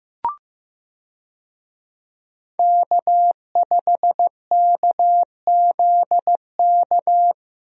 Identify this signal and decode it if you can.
Morse code, heard as a keyed tone, about 15 wpm, 700 Hz. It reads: K5KZK
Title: USA Call Signs - 5 Characters - Repeated 3x - 15wpm